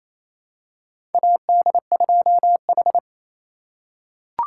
Morse code, heard as a keyed tone, about 28 wpm, 700 Hz. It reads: AD2H
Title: USA Call Signs - 4 Characters - 3x Character Spacing - 28wpm